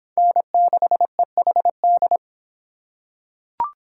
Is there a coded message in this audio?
Yes: Morse code from 0.2 to 2.2 s, heard as a keyed tone, about 22 wpm, 700 Hz. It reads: N6EHD